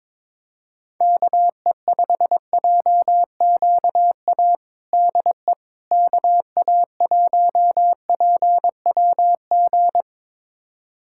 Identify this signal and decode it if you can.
Morse code, heard as a keyed tone, about 22 wpm, 700 Hz. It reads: KE5JQA DE KA1PWG
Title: USA Callsign DE Callsign 22wpm